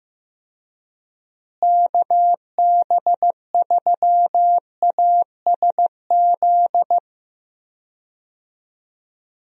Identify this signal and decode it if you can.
Morse code, heard as a keyed tone, about 15 wpm, 700 Hz. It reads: KB3ASZ
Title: USA Call Signs - 6 Characters - Repeated 2x - 15wpm